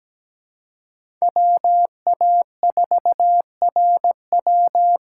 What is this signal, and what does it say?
Morse code, heard as a keyed tone, about 17 wpm, 700 Hz. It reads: WA4RW
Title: USA Call Signs - 5 Characters - 2x Character Spacing - 17wpm